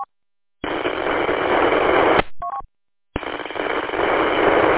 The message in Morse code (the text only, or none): none